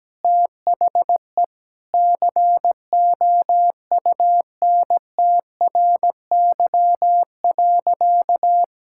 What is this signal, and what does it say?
Morse code, heard as a keyed tone, about 17 wpm, 700 Hz. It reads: THE COUNTRY.
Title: Sentences from Top 1000 Words - Review Part 6 - 17wpm